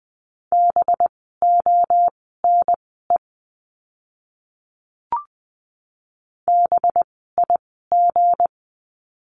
Morse code, heard as a keyed tone, about 20 wpm, spaced as for 14 wpm, 700 Hz. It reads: BONE BIG
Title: 185 Lesson 30f - Letter b - Words 20wpm